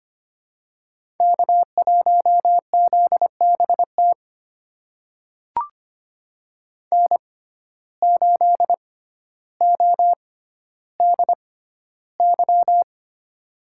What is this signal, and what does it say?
Morse code, heard as a keyed tone, about 25 wpm, spaced as for 13 wpm, 700 Hz. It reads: K1ZBT N8ODY
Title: USA Call Signs - 5 Characters - 6x Character Spacing - 25wpm